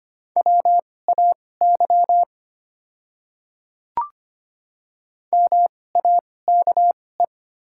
Morse code, heard as a keyed tone, about 25 wpm, spaced as for 18 wpm, 690 Hz. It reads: WAY MAKE
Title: 2x Character Spacing - Top 200 Words - 25wpm